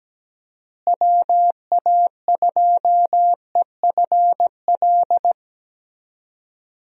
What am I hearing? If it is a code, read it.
Morse code, heard as a keyed tone, about 17 wpm, 700 Hz. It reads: WA2EFL
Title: USA Call Signs - 6 Characters - Repeated 3x - 17wpm